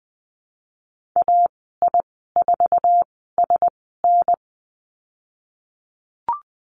Morse code, heard as a keyed tone, about 20 wpm, spaced as for 15 wpm, 700 Hz. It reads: AI4SN
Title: 048 Lesson 9g - Number 4 - Call Signs 20wpm